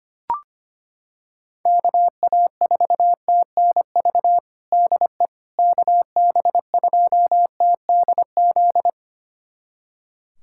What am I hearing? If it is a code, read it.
Morse code, heard as a keyed tone, about 25 wpm, 700 Hz. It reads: KA4TNV DE KB2TDZ
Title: USA Callsign DE Callsign - 6 Characters - 25wpm